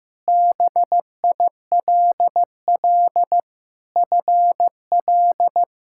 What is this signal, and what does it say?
Morse code, heard as a keyed tone, about 16 wpm, 700 Hz. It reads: BILL FL
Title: K1USN SST QSO Practice - 15wpm